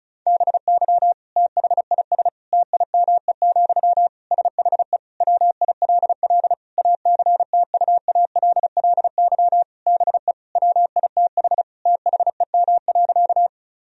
Morse code, heard as a keyed tone, about 34 wpm, 700 Hz. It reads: BY THIS TIME, SHE WILL ACTUALLY BE WITH THEM.
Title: Sentences from Top 1000 Words - Review Part 4 - 35wpm